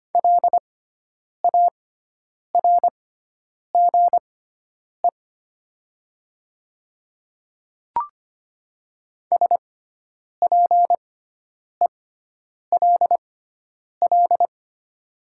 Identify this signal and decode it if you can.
Morse code, heard as a keyed tone, about 25 wpm, spaced as for 9 wpm, 700 Hz. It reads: LARGE SPELL
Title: 6x Character Spacing - Top 200 Words - 25wpm